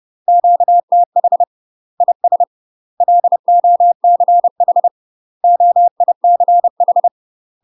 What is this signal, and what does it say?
Morse code, heard as a keyed tone, about 30 wpm, 700 Hz. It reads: QTH IS LOCH OICH
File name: UK MASTER QSOs TRAINING / 30WPM / UK MASTER QSOs 0043